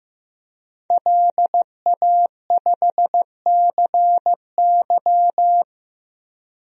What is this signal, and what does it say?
Morse code, heard as a keyed tone, about 15 wpm, 700 Hz. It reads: LA5CY
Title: DXCC-100 Call Signs - 5 Characters - Repeated 3x - 15wpm